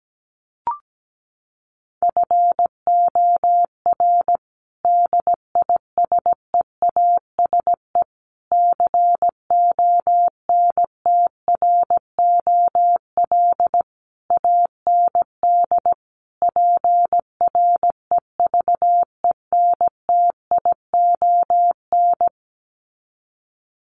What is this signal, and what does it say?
Morse code, heard as a keyed tone, about 17 wpm, 700 Hz. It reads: FOR DISEASE CONTROL AND PREVENTION
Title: Sets of 5 Words Encore2 17wpm